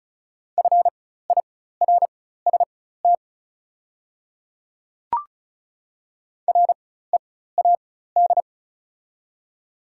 Morse code, heard as a keyed tone, about 36 wpm, spaced as for 16 wpm, 690 Hz. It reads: FIRST READ
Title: 4x Character Spacing - Top 200 Words - 35wpm